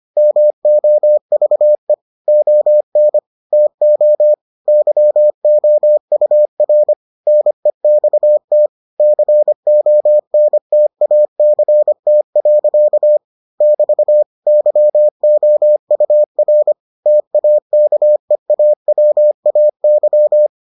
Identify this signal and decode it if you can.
Morse code, heard as a keyed tone, about 25 wpm, 600 Hz. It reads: MOVE ON TO YOUR NEXT CONTACT. <BT> YOUR TAKEAWAY